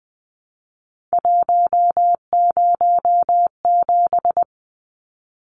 Morse code, heard as a keyed tone, about 20 wpm, 700 Hz. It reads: 107
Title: Numbers - 3-Digit 20wpm